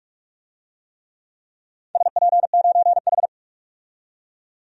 Morse code, heard as a keyed tone, about 41 wpm, 700 Hz. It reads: SP9H
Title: DXCC-100 Call Signs - 4 Characters - 45wpm